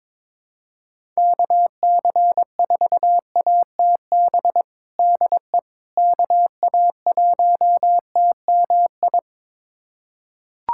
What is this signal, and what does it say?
Morse code, heard as a keyed tone, about 22 wpm, 700 Hz. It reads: KC4ATB DE KA1TMI
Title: USA Callsign DE Callsign - 6 Characters - 22wpm